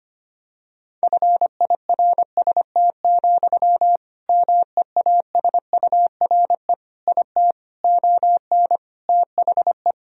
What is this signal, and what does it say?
Morse code, heard as a keyed tone, about 25 wpm, 700 Hz. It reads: FIRST, MEASURE IT ON THE
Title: Sentences from Top 1000 Words - Review Part 3 - 25wpm